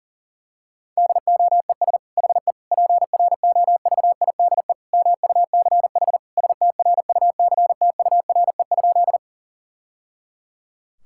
Morse code, heard as a keyed tone, about 40 wpm, 700 Hz. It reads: DOES HE PROVIDE MUCH STRUCTURE?
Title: Sentences from Top 1000 Words - Review Part 6 - 40wpm